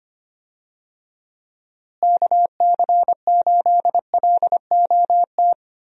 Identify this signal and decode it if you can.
Morse code, heard as a keyed tone, about 25 wpm, 700 Hz. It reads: KC8LOT
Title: USA Call Signs - 6 Characters - Repeated 2x - 25wpm